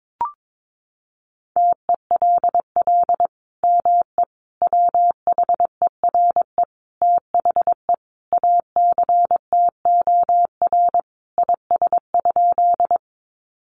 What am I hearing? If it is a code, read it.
Morse code, heard as a keyed tone, about 22 wpm, 700 Hz. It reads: TELL ME WHERE THE ACTOR IS?
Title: Sentences from Top 1500 Words 22wpm